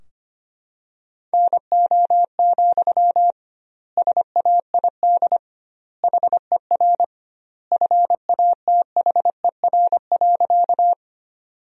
Morse code, heard as a keyed tone, about 25 wpm, 700 Hz. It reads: NO, SAID HER FATHER.
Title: 2x Word-Spacing - Sentences from Top 300 Words - 25wpm